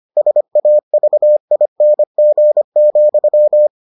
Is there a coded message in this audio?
Yes: Morse code from 0.2 to 3.7 s, heard as a keyed tone, about 25 wpm, 600 Hz. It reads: SAVING,